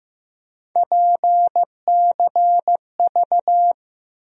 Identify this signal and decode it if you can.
Morse code, heard as a keyed tone, about 15 wpm, 700 Hz. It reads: PCV
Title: USA Call Sign Postfixes - 3 Letters 15wpm